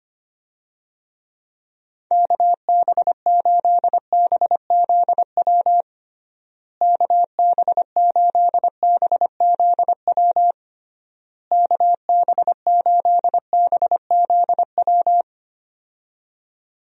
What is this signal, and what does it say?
Morse code, heard as a keyed tone, about 25 wpm, 700 Hz. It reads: KB8BZW KB8BZW KB8BZW
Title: USA Call Signs - 6 Characters - Repeated 3x - 25wpm